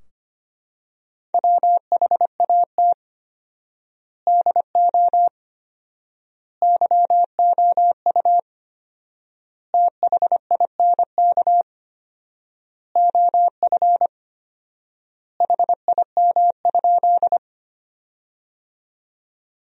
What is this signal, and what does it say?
Morse code, heard as a keyed tone, about 25 wpm, 700 Hz. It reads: WHAT DO YOU THINK OF HIM?
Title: 4x Word-Spacing - Sentences from Top 200 Words - 25wpm